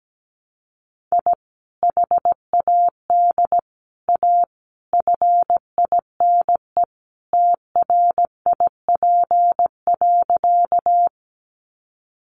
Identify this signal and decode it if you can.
Morse code, heard as a keyed tone, about 17 wpm, 700 Hz. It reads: I HAD A FINE TRIP.
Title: Sentences from Top 1300 Words 17wpm